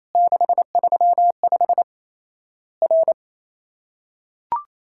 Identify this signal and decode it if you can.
Morse code, heard as a keyed tone, about 28 wpm, 700 Hz. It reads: 635 R
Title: Worked All Europe DX Contest - QTC Exchange - 28wpm